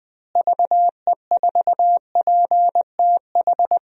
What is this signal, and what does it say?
Morse code, heard as a keyed tone, about 20 wpm, 700 Hz. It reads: VE4PTH
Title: DXCC-100 Call Signs - 6 Characters - Repeated 2x - 20wpm